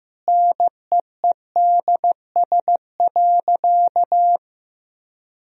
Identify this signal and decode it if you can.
Morse code, heard as a keyed tone, about 16 wpm, 700 Hz. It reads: NEEDS.
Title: Sentences from Top 1000 Words - Review Part 3 - 15wpm